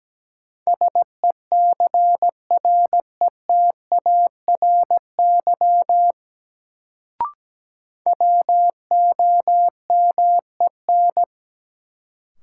Morse code, heard as a keyed tone, about 17 wpm, 700 Hz. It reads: SECRETARY WOMEN
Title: Sentences from Top 1300 Words - New Word Drill Alt - 17wpm